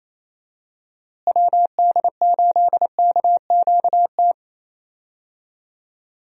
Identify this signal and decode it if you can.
Morse code, heard as a keyed tone, about 28 wpm, 700 Hz. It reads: WD8KQT
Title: USA Call Signs - 6 Characters - Repeated 3x - 28wpm